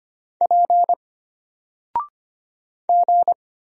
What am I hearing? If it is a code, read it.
Morse code, heard as a keyed tone, about 25 wpm, 700 Hz. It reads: P G
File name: Single-Letter-Number-25wpm